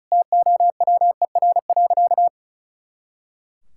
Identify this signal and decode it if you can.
Morse code, heard as a keyed tone, about 36 wpm, 700 Hz. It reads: TOWER.
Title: Pangram Challenge v2 - 35wpm